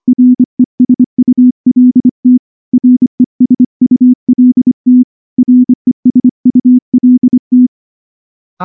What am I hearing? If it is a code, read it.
Morse code, heard as a keyed tone, about 24 wpm, 250 Hz. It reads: RESULT RESULT RESULT
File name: morse / morse-training-25WPM